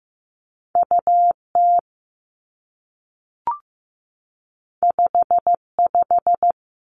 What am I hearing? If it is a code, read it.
Morse code, heard as a keyed tone, about 15 wpm, 700 Hz. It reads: U T 5 5
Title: Two Challenging Letter-Number Combinations 15wpm